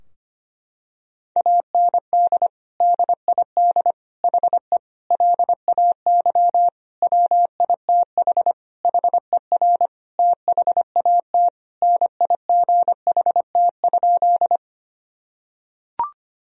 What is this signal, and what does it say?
Morse code, heard as a keyed tone, about 25 wpm, 700 Hz. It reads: AND DID HE LAY WITH HER THAT NIGHT?
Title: Sentences from Top 1000 Words - Review Part 2 - 25wpm